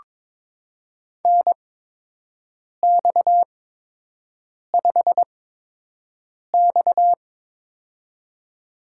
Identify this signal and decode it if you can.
Morse code, heard as a keyed tone, about 21 wpm, spaced as for 6 wpm, 700 Hz. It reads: NX5X
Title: USA Call Signs - 4 Characters - 8x Character Spacing - 22wpm